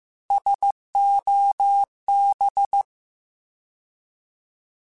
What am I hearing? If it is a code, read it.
Morse code, heard as a keyed tone, about 15 wpm, 800 Hz. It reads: SOB
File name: Alphabetical / Alphabetical mp3s / Ss @13